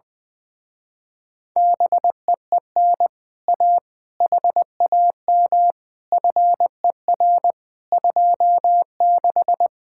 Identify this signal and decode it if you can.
Morse code, heard as a keyed tone, about 20 wpm, 700 Hz. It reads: BEEN A HAM FER 26
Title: Speed-Racing Common QSO Phrases - 20wpm